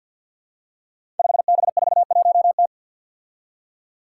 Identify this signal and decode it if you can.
Morse code, heard as a keyed tone, about 52 wpm, 700 Hz. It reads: 5B41T
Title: Five Challenging Letter-Number Combinations 50wpm